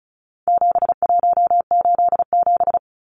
Morse code, heard as a keyed tone, about 34 wpm, 700 Hz. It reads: 7187
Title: Numbers - 4-Digit 35wpm